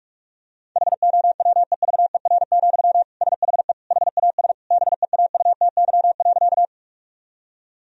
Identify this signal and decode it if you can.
Morse code, heard as a keyed tone, about 41 wpm, 700 Hz. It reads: HOWEVER, SHE HAS BEAUTY.
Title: Sentences from Top 1000 Words - Review Part 2 - 45wpm